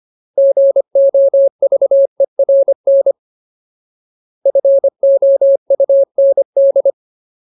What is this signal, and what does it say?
Morse code, heard as a keyed tone, about 25 wpm, 550 Hz. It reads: GOVERN FOUND
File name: Ultimate Word List / 25WPM Ultimate